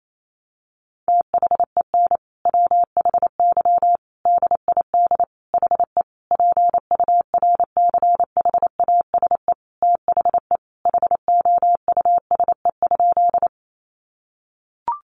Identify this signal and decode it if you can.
Morse code, heard as a keyed tone, about 28 wpm, 700 Hz. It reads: THEN WHY DID HE PURCHASE THE HOUSE?
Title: Sentences from Top 800 Words 28wpm